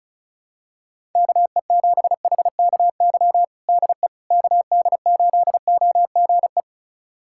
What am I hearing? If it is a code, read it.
Morse code, heard as a keyed tone, about 34 wpm, 700 Hz. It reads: KE7HKY DE KD8OGE
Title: USA Callsign DE Callsign - 6 Characters - 35wpm